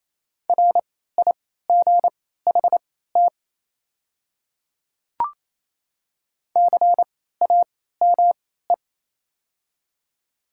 Morse code, heard as a keyed tone, about 28 wpm, spaced as for 16 wpm, 700 Hz. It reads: RIGHT CAME
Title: 3x Character Spacing - Top 200 Words - 28wpm